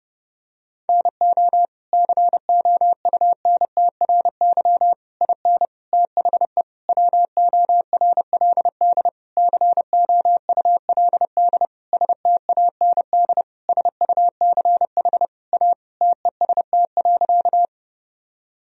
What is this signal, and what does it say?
Morse code, heard as a keyed tone, about 32 wpm, 700 Hz. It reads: NO COUNTRY IN THE WORLD COULD STAND SUCH A TEST.
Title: Sentences from Top 1000 Words - Review Part 3 - 30wpm